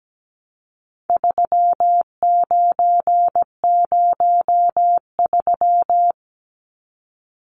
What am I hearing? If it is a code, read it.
Morse code, heard as a keyed tone, about 17 wpm, 700 Hz. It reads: 3903
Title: Numbers - 4-Digit 17wpm